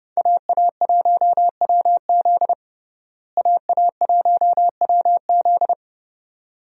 Morse code, heard as a keyed tone, about 30 wpm, 700 Hz. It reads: AA1WZ AA1WZ
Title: USA Call Signs - 5 Characters - Repeated 2x - 30wpm